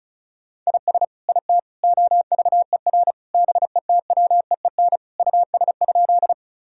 Morse code, heard as a keyed tone, about 36 wpm, 700 Hz. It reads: IS IT OVER BETWEEN US?
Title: Sentences from Top 1000 Words - Review Part 1 - 35wpm